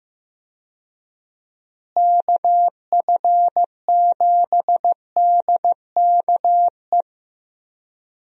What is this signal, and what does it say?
Morse code, heard as a keyed tone, about 15 wpm, 700 Hz. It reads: KF7DKE
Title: USA Call Signs - 6 Characters - Repeated 3x - 15wpm